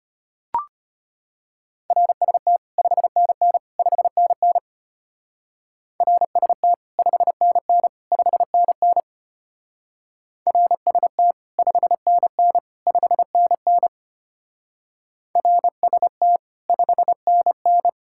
Morse code, heard as a keyed tone, about 32 wpm, 700 Hz. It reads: RST 5NN 5NN RST 5NN 5NN RST 5NN 5NN RST 5NN